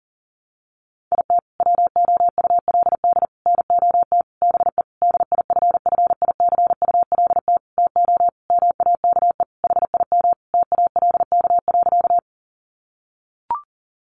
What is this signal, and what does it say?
Morse code, heard as a keyed tone, about 40 wpm, 700 Hz. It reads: IT WOULD NOT BE DIFFICULT TO MAKE HIM TALK.
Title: Sentences from Top 600 Words 40wpm